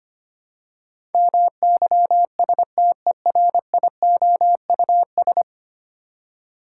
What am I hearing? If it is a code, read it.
Morse code, heard as a keyed tone, about 25 wpm, 700 Hz. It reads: MYSTERIOUS
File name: Ten Letter Words - 25wpm